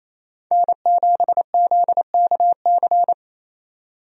Morse code, heard as a keyed tone, about 27 wpm, 700 Hz. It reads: N7ZKC
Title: USA Call Signs - 5 Characters - 28wpm